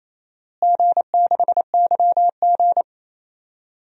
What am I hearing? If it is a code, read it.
Morse code, heard as a keyed tone, about 28 wpm, 700 Hz. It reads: G6YG
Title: DXCC-100 Call Signs - 4 Characters - Repeated 2x - 28wpm